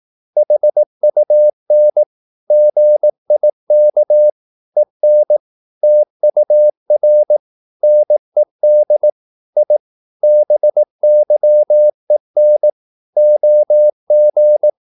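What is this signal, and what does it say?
Morse code, heard as a keyed tone, about 18 wpm, 600 Hz. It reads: HUN GIK EN TUR NED I BYEN OG
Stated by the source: Slow (QRS) Morse practice at 18 wpm, CW med Gnister.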